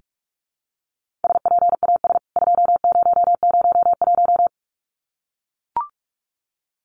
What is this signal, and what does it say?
Morse code, heard as a keyed tone, about 44 wpm, 700 Hz. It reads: SPAS 2001
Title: CWT Exchange 45wpm